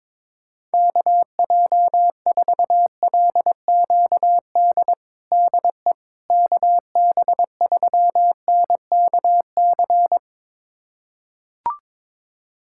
Morse code, heard as a keyed tone, about 22 wpm, 700 Hz. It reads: KJ4LQD DE KB3NKC
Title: USA Callsign DE Callsign - 6 Characters - 22wpm